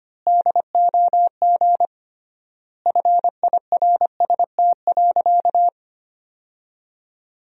Morse code, heard as a keyed tone, about 25 wpm, 700 Hz. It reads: DOG FIRST.
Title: 3x Word-Spacing - Sentences from Top 400 Words - 25wpm